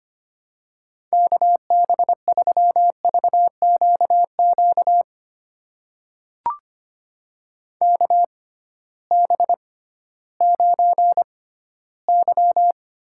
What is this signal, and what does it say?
Morse code, heard as a keyed tone, about 25 wpm, spaced as for 15 wpm, 700 Hz. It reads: KB3VQQ KB9Y
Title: USA Call Signs - 6x Character Spacing - 25wpm